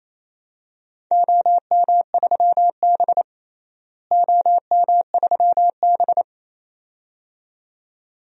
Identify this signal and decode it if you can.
Morse code, heard as a keyed tone, about 28 wpm, 700 Hz. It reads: OM3B OM3B
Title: DXCC-100 Call Signs - 4 Characters - Repeated 2x - 28wpm